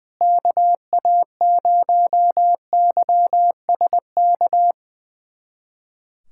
Morse code, heard as a keyed tone, about 21 wpm, 700 Hz. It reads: KA0YSK
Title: USA Callsign DE Callsign 20wpm